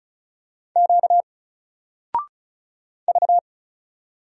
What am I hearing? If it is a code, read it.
Morse code, heard as a keyed tone, about 35 wpm, 700 Hz. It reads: Q V
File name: Single-Letter-Number-35wpm